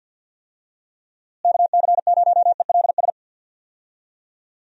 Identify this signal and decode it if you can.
Morse code, heard as a keyed tone, about 51 wpm, 700 Hz. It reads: KC0ELS
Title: USA Call Signs - 6 Characters - Repeated 3x - 50wpm